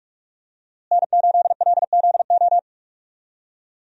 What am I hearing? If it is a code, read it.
Morse code, heard as a keyed tone, about 42 wpm, 700 Hz. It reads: N8LZO
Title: USA Call Signs - 5 Characters - Repeated 3x - 45wpm